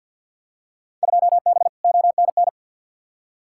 Morse code, heard as a keyed tone, about 48 wpm, 700 Hz. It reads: 2B ONN
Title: Field Day Exchanges Encore - 50wpm